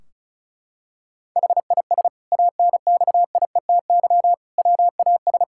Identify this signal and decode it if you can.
Morse code, heard as a keyed tone, about 36 wpm, 700 Hz. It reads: HIS ANXIETY WAS
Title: Sentences from Top 1600 Words 35wpm